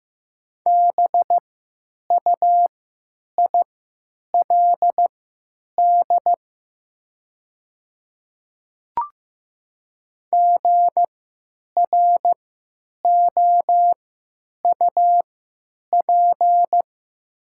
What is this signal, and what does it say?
Morse code, heard as a keyed tone, about 15 wpm, spaced as for 9 wpm, 700 Hz. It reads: BUILD GROUP
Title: 3x Character Spacing - Top 500 Words - 15wpm